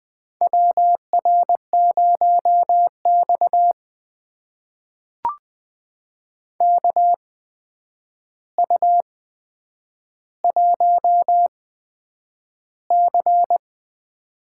Morse code, heard as a keyed tone, about 20 wpm, spaced as for 9 wpm, 700 Hz. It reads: WR0X KU1C
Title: USA Call Signs - 4 Characters - 8x Character Spacing - 20wpm